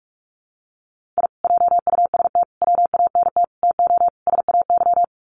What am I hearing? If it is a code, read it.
Morse code, heard as a keyed tone, about 44 wpm, 700 Hz. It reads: I JUST WANT TO SAY
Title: Sets of 5 Words Encore 45wpm